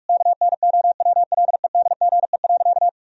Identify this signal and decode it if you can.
Morse code, heard as a keyed tone, about 44 wpm, 700 Hz. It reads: KNOWLEDGE.
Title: Sentences from Top 1000 Words - Review Part 3 - 45wpm